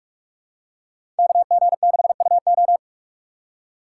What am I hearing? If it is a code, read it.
Morse code, heard as a keyed tone, about 46 wpm, 700 Hz. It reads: KG6UO
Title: USA Call Signs - 5 Characters - 4x Character Spacing - 45wpm